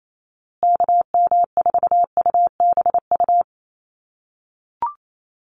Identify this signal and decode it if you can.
Morse code, heard as a keyed tone, about 28 wpm, 700 Hz. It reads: KM4UBU
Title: USA Call Signs Repeated 3x 28wpm